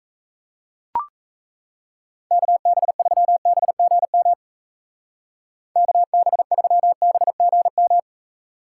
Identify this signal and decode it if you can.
Morse code, heard as a keyed tone, about 34 wpm, 700 Hz. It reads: KB3BGM KB3BGM
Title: Speed Racing - USA 2x3 Call Signs 28wpm